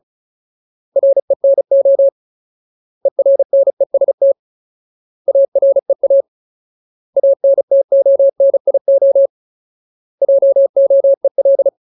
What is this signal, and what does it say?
Morse code, heard as a keyed tone, about 34 wpm, 550 Hz. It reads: RENO ERNEST AREA ANTONIO JOEL
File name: Ultimate Word List / 35WPM Ultimate